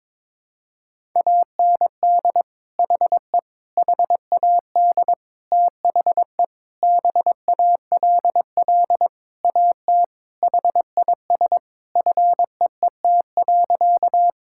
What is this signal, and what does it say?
Morse code, heard as a keyed tone, about 22 wpm, 700 Hz. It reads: AND HE HAD THE BALL AT HIS FEET.
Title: Sentences from Top 1000 Words - Review Part 6 - 22wpm